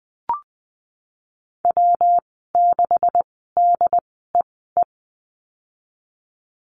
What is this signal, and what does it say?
Morse code, heard as a keyed tone, about 20 wpm, spaced as for 15 wpm, 700 Hz. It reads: W6DEE
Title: 147 Lesson 24g - Letter y - Call Signs 20wpm